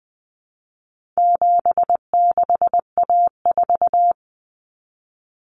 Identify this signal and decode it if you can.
Morse code, heard as a keyed tone, about 20 wpm, 700 Hz. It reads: 76A4
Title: Common Cut Numbers - 4-Digit 20wpm